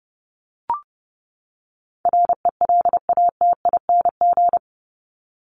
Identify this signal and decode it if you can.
Morse code, heard as a keyed tone, about 30 wpm, 700 Hz. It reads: RELATING
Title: ING Suffix - Top 100 Words with Base Words 30wpm